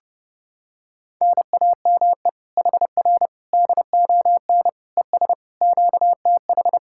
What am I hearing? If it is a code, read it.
Morse code, heard as a keyed tone, about 30 wpm, 700 Hz. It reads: NAME HR DON ES QTH